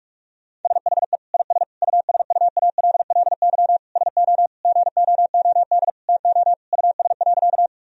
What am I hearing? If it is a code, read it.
Morse code, heard as a keyed tone, about 46 wpm, 700 Hz. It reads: SHE IS USUALLY SO GOOD TO US.
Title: Sentences from Top 1000 Words - Review Part 3 - 45wpm